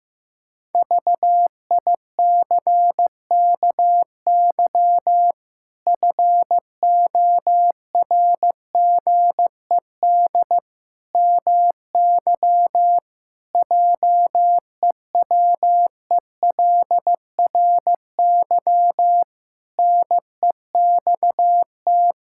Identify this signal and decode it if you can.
Morse code, heard as a keyed tone, about 15 wpm, 700 Hz. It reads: VICKY FORGED MY JEWELRY NEXT
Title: Pangram Challenge v2 - 15wpm